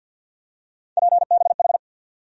Morse code, heard as a keyed tone, about 48 wpm, 700 Hz. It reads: PBH
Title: USA Call Sign Postfixes - 3 Letters 50wpm